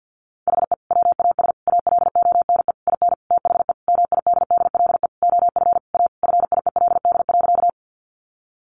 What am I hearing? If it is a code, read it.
Morse code, heard as a keyed tone, about 50 wpm, 700 Hz. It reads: HE WAS ALONE IN THE MIDDLE OF A FIELD.
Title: Sentences from Top 1000 Words 50wpm